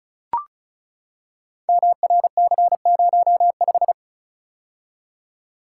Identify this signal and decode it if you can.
Morse code, heard as a keyed tone, about 36 wpm, 690 Hz. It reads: MRC05
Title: Code Groups - 5 Letter:Numbers - 35wpm